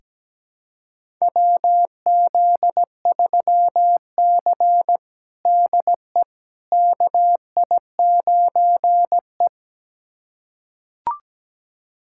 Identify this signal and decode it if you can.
Morse code, heard as a keyed tone, about 17 wpm, 700 Hz. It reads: WZ3C DE KI9E
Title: USA Callsign DE Callsign - 4 Characters - 17wpm